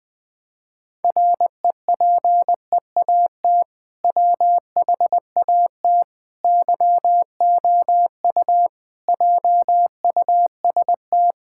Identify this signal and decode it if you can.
Morse code, heard as a keyed tone, about 21 wpm, 700 Hz. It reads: REPEAT WHAT YOU JUST
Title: Sentences from Top 1600 Words 20wpm